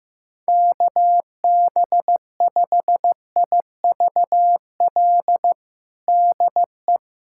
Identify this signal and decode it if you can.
Morse code, heard as a keyed tone, about 15 wpm, 700 Hz. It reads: KB5IVL DE
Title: USA Callsign DE Callsign - 6 Characters - 15wpm